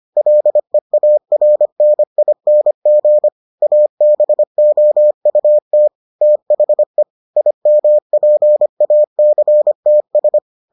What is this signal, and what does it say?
Morse code, heard as a keyed tone, about 25 wpm, 600 Hz. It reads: LEARNING ABOUT THE IMPACTS